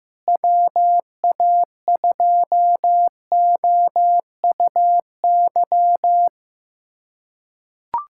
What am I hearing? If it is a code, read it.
Morse code, heard as a keyed tone, about 15 wpm, 700 Hz. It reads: WA2OUY
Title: USA Callsign DE Callsign 15wpm